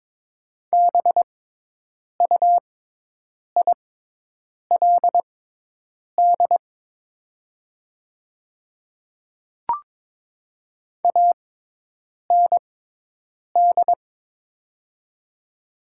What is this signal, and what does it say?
Morse code, heard as a keyed tone, about 22 wpm, spaced as for 8 wpm, 700 Hz. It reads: BUILD AND
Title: 6x Character Spacing - Top 500 Words - 22wpm